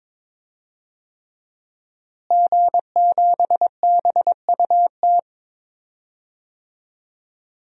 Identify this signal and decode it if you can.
Morse code, heard as a keyed tone, about 22 wpm, 700 Hz. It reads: G7BUT
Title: DXCC-100 Call Signs - 5 Characters - Repeated 3x - 22wpm